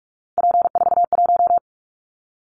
Morse code, heard as a keyed tone, about 46 wpm, 700 Hz. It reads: P41
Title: Callsign Prefixes for Top 100 DXCC - 45wpm